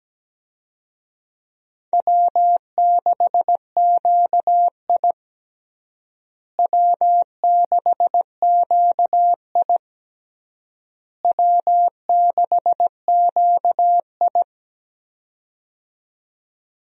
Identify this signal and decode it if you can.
Morse code, heard as a keyed tone, about 17 wpm, 700 Hz. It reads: W6QI W6QI W6QI
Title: USA Call Signs - 4 Characters - Repeated 3x - 17wpm